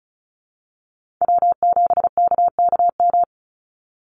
Morse code, heard as a keyed tone, about 34 wpm, 700 Hz. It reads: W7KKM
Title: NA SOTA Chaser Callsigns 35wpm